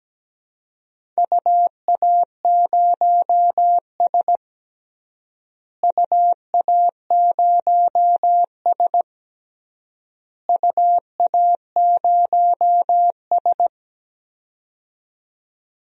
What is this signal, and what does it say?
Morse code, heard as a keyed tone, about 17 wpm, 700 Hz. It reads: UA0S UA0S UA0S
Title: DXCC-100 Call Signs - 4 Characters - Repeated 3x - 17wpm